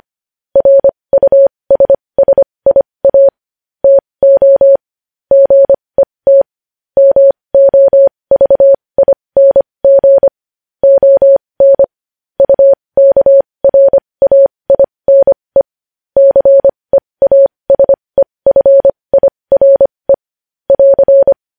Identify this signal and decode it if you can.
Morse code, heard as a keyed tone, about 25 wpm, spaced as for 20 wpm, 550 Hz. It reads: RUSSIA TO GET MOVING ON UKRAINE CEASEFIRE <AR>
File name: podcast / News headlines-1x-25x20